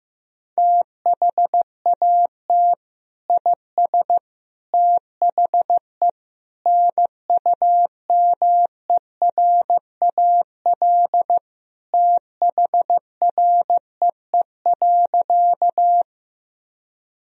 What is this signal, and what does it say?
Morse code, heard as a keyed tone, about 15 wpm, 700 Hz. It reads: THAT IS THE NUMERAL THREE.
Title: Sentences from Top 1000 Words - Review Part 2 - 15wpm